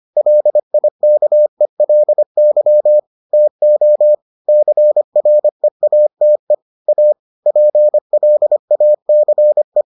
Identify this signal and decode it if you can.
Morse code, heard as a keyed tone, about 25 wpm, 600 Hz. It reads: LIKELY TO CREATE A PLACE